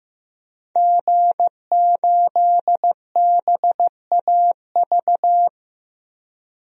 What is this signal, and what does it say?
Morse code, heard as a keyed tone, about 15 wpm, 700 Hz. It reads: G8BAV
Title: DXCC-100 Call Signs - 5 Characters - 15wpm